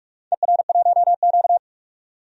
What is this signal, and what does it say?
Morse code, heard as a keyed tone, about 46 wpm, 700 Hz. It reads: ER1Q
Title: DXCC-100 Call Signs - 4 Characters - 45wpm